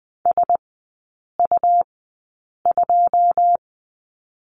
Morse code, heard as a keyed tone, about 20 wpm, spaced as for 8 wpm, 700 Hz. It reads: SU2
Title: 218 Lesson 35e - Number 0 - Three Characters 20wpm